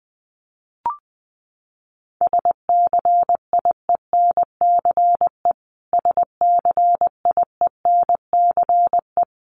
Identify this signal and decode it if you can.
Morse code, heard as a keyed tone, about 19 wpm, 700 Hz. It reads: SCIENCE SCIENCE
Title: Top 500 Words Repeated 20wpm